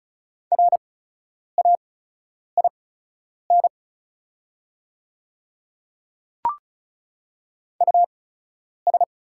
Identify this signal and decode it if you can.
Morse code, heard as a keyed tone, about 36 wpm, spaced as for 10 wpm, 700 Hz. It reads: RAIN US
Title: 8x Character Spacing - Top 300 Words - 35wpm